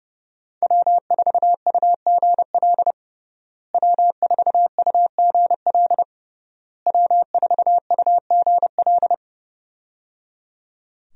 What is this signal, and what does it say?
Morse code, heard as a keyed tone, about 30 wpm, 700 Hz. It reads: W4UGL W4UGL W4UGL
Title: USA Call Signs - 5 Characters - Repeated 3x - 30wpm